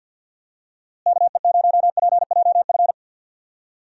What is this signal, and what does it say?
Morse code, heard as a keyed tone, about 50 wpm, 700 Hz. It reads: KE0PJF
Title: USA Call Signs - 6 Characters - Repeated 3x - 50wpm